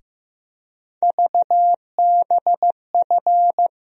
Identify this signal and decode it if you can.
Morse code, heard as a keyed tone, about 15 wpm, 700 Hz. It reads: VBF
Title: USA Call Sign Postfixes - 3 Letters 15wpm